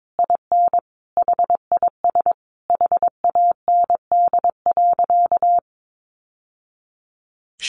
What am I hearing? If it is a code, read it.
Morse code, heard as a keyed tone, about 22 wpm, 700 Hz. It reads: IN HIS HAND.
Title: Sentences from Top 600 Words 22wpm